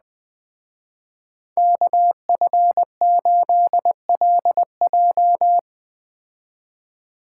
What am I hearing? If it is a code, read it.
Morse code, heard as a keyed tone, about 21 wpm, 700 Hz. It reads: KF8LJ